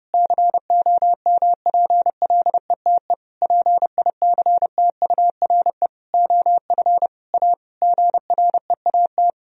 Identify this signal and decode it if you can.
Morse code, heard as a keyed tone, about 30 wpm, 700 Hz. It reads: COMPLETE PICTURE OF A GREAT
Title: Sentences from Top 1000 Words - Review Part 2 - 30wpm